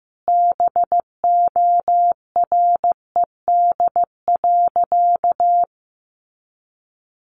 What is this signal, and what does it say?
Morse code, heard as a keyed tone, about 15 wpm, 700 Hz. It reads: BORED.
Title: Easy Phrases 15wpm